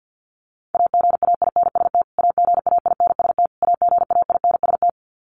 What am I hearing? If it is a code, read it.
Morse code, heard as a keyed tone, about 50 wpm, 700 Hz. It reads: AGAINST AGAINST AGAINST
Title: Top 500 Words Repeated 50wpm